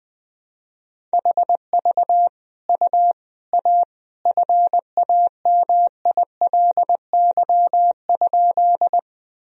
Morse code, heard as a keyed tone, about 20 wpm, 700 Hz. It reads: HV U A FAMILY?
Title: Sentences from Top 400 Words and Abbreviations 20wpm